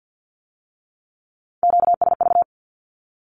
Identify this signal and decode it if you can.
Morse code, heard as a keyed tone, about 50 wpm, 700 Hz. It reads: QSV
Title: Q-Codes with Spoken Meaning - 50wpm